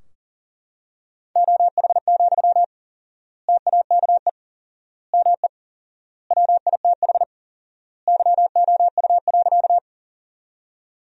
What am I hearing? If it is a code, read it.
Morse code, heard as a keyed tone, about 40 wpm, 700 Hz. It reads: OH, TAKE ME WITH YOU.
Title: 4x Word-Spacing - Sentences from Top 500 Words - 40wpm